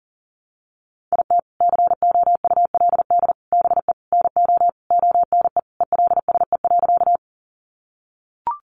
Morse code, heard as a keyed tone, about 37 wpm, 700 Hz. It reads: IT COULD BE NO ONE ELSE.
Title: Sentences from Top 800 Words 40wpm